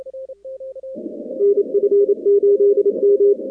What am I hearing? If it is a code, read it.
Morse code, heard as a keyed tone, about 27 wpm, 400 Hz. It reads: NF8M
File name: contests / 2019-cw-wpx-kt5j-k5pi- -2330 0-46